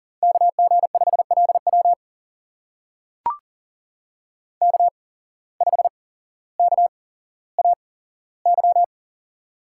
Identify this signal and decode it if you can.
Morse code, heard as a keyed tone, about 39 wpm, spaced as for 17 wpm, 700 Hz. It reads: KG5LW K5KAY
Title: USA Call Signs - 5 Characters - 8x Character Spacing - 40wpm